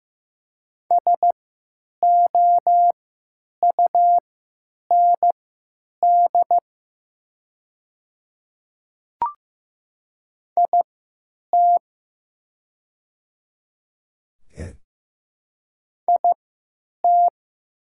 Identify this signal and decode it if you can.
Morse code, heard as a keyed tone, about 15 wpm, spaced as for 9 wpm, 700 Hz. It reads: SOUND IT IT
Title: 3x Character Spacing - Top 100 Words - 15wpm